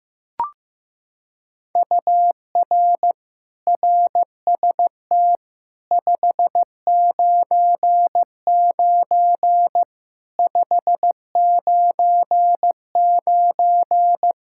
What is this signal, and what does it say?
Morse code, heard as a keyed tone, about 15 wpm, 700 Hz. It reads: UR RST 599 599
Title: Practice QSO - Exchange 1 - Beginning - 15wpm